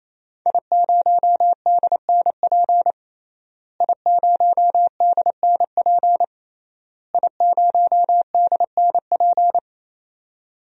Morse code, heard as a keyed tone, about 28 wpm, 700 Hz. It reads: I0DNP I0DNP I0DNP
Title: DXCC-100 Call Signs - 5 Characters - Repeated 3x - 28wpm